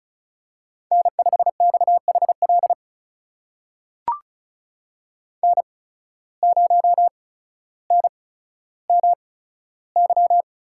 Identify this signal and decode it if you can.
Morse code, heard as a keyed tone, about 36 wpm, spaced as for 15 wpm, 700 Hz. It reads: N5XHL N0NMY
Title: USA Call Signs - 5 Characters - 8x Character Spacing - 35wpm